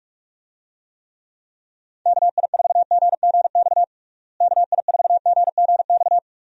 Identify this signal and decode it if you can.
Morse code, heard as a keyed tone, about 46 wpm, 700 Hz. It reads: KI4GGX KI4GGX
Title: USA Call Signs - 6 Characters - Repeated 3x - 45wpm